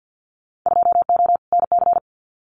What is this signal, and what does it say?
Morse code, heard as a keyed tone, about 50 wpm, 700 Hz. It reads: 2O NC
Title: Winter Field Day Exchanges 50wpm